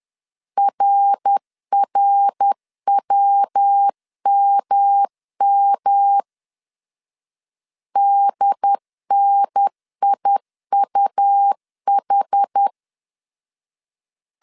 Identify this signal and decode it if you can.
Morse code, heard as a keyed tone, about 10 wpm, 800 Hz. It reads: RRWMM DNIUH
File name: CW / DET / 1 3 1500C S B 10wpm 3335